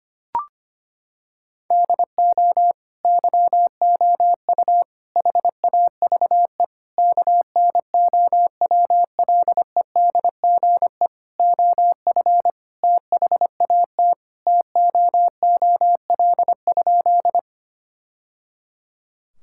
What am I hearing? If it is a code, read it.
Morse code, heard as a keyed tone, about 25 wpm, 700 Hz. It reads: DO YOU HAVE KNOWLEDGE OF THAT TOOL?
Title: Sentences from Top 1000 Words - Review Part 6 - 25wpm